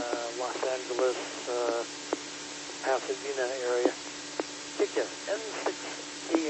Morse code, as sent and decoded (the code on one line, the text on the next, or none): none